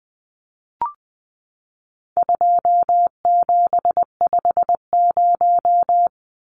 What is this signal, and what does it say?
Morse code, heard as a keyed tone, about 20 wpm, 700 Hz. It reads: 2750
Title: Numbers - 4-Digit 20wpm